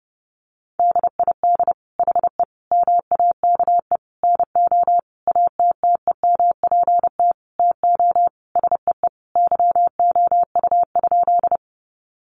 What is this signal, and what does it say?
Morse code, heard as a keyed tone, about 30 wpm, 700 Hz. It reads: DID HE MAKE NO ATTEMPT TO SEE YOU?
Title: Sentences from Top 1000 Words 30wpm